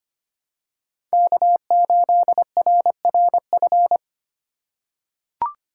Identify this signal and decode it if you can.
Morse code, heard as a keyed tone, about 25 wpm, 700 Hz. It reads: K8RRF
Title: USA Call Signs - 5 Characters - 3x Character Spacing - 25wpm